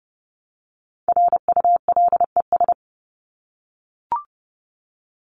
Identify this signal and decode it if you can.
Morse code, heard as a keyed tone, about 30 wpm, 700 Hz. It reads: RULES
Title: ES:S Suffix - Top 100 Words with Base Words 30wpm